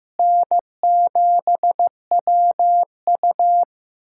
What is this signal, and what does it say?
Morse code, heard as a keyed tone, about 16 wpm, 700 Hz. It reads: N7WU
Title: USA Callsign DE Callsign - 4 Characters - 15wpm